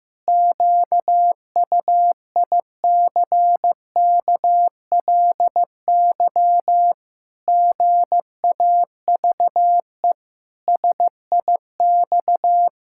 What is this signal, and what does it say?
Morse code, heard as a keyed tone, about 14 wpm, 700 Hz. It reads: QUICKLY GAVE SIX
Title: Pangram Challenge v2 - 15wpm